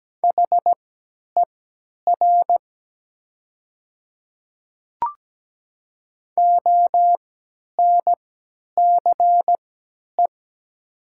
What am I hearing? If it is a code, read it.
Morse code, heard as a keyed tone, about 17 wpm, spaced as for 10 wpm, 700 Hz. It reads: HER ONCE
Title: 3x Character Spacing - Top 300 Words - 17wpm